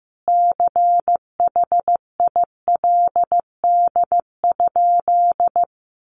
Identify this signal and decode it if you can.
Morse code, heard as a keyed tone, about 15 wpm, 700 Hz. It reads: CHILD?
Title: Sentences from Top 1000 Words 15wpm